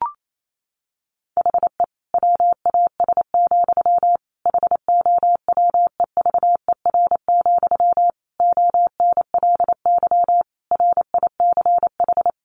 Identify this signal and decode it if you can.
Morse code, heard as a keyed tone, about 28 wpm, 700 Hz. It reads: HE WAS, HOWEVER, ONLY RICH
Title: Sentences from Top 1100 Words 28wpm